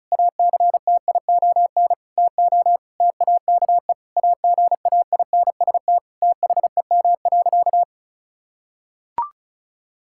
Morse code, heard as a keyed tone, about 34 wpm, 700 Hz. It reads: ACTION TO TAKE AGAINST THEM.
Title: Sentences from Top 1000 Words - Review Part 6 - 35wpm